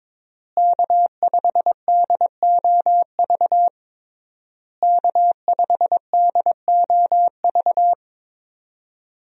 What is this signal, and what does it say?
Morse code, heard as a keyed tone, about 22 wpm, 700 Hz. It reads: K5DOV K5DOV
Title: USA Call Signs - 5 Characters - Repeated 3x - 22wpm